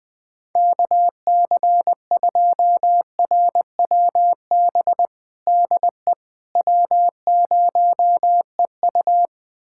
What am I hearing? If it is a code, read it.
Morse code, heard as a keyed tone, about 20 wpm, 700 Hz. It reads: KC2RWB DE W0EU
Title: USA Callsign DE Callsign 20wpm